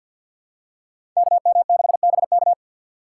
Morse code, heard as a keyed tone, about 53 wpm, 700 Hz. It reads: KM6BK
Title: USA Call Signs - 5 Characters - 3x Character Spacing - 50wpm